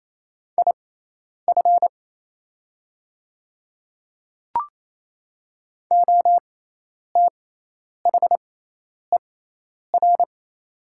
Morse code, heard as a keyed tone, about 28 wpm, spaced as for 10 wpm, 700 Hz. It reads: IF OTHER
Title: 6x Character Spacing - Top 100 Words - 28wpm